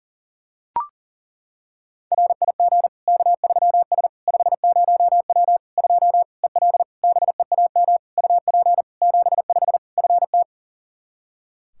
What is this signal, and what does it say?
Morse code, heard as a keyed tone, about 40 wpm, 700 Hz. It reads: RIG K3S 50W 2 EL BEAM UP 75 FT
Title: Practice QSO - Exchange 2 - Info - 40wpm